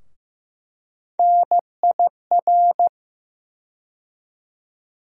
Morse code, heard as a keyed tone, about 15 wpm, 700 Hz. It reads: NIR
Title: USA Call Sign Postfixes - 3 Letters 15wpm